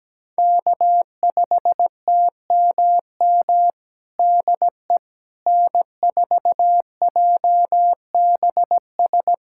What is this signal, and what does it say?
Morse code, heard as a keyed tone, about 17 wpm, 700 Hz. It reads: K5TMM DE N4JBS
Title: USA Callsign DE Callsign - 5 Characters - 17wpm